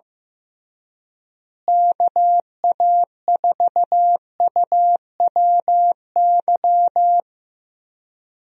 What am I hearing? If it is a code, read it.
Morse code, heard as a keyed tone, about 15 wpm, 700 Hz. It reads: KA4UWY
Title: USA Call Signs - 6 Characters - Repeated 3x - 15wpm